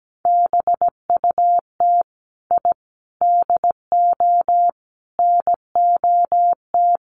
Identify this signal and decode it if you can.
Morse code, heard as a keyed tone, about 17 wpm, 700 Hz. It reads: BUT I DO NOT
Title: Sets of 5 Words 17wpm